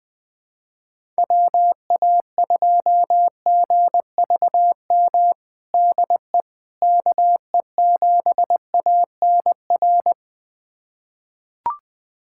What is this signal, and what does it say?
Morse code, heard as a keyed tone, about 20 wpm, 700 Hz. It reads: WA2GVM DE KE7ANR
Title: USA Callsign DE Callsign - 6 Characters - 20wpm